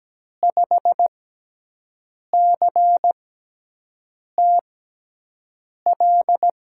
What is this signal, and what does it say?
Morse code, heard as a keyed tone, about 17 wpm, spaced as for 6 wpm, 700 Hz. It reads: 5CTL
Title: USA Call Signs - 6x Character Spacing - 17wpm